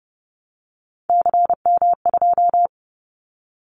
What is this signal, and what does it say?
Morse code, heard as a keyed tone, about 30 wpm, 700 Hz. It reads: CM2
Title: Callsign Prefixes for Top 100 DXCC - 30wpm